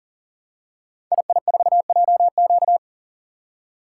Morse code, heard as a keyed tone, about 39 wpm, 700 Hz. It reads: II4JQ
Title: Code Groups - 5 Letter:Numbers - 40wpm